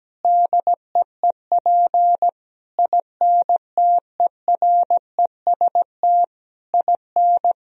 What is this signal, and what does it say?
Morse code, heard as a keyed tone, about 17 wpm, 700 Hz. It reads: DEEP INTEREST IN
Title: Sentences from Top 1000 Words - Review Part 3 - 17wpm